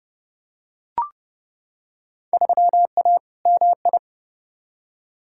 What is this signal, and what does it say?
Morse code, heard as a keyed tone, about 30 wpm, 700 Hz. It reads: 3A MI
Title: Field Day Exchanges - 30wpm